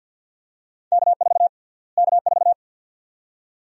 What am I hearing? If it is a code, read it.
Morse code, heard as a keyed tone, about 50 wpm, 700 Hz. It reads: K4 K4
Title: USA Call Sign Prefixes - Repeated 2x - 50wpm